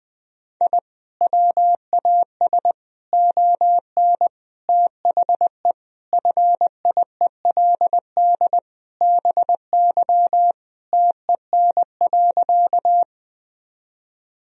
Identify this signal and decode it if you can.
Morse code, heard as a keyed tone, about 20 wpm, 700 Hz. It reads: I WAS ON THE FIELD BY TEN.
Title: Sentences from Top 1000 Words - Review Part 2 - 20wpm